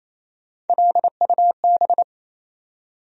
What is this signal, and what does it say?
Morse code, heard as a keyed tone, about 27 wpm, 700 Hz. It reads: LUB
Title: USA Call Sign Postfixes - 3 Letters 28wpm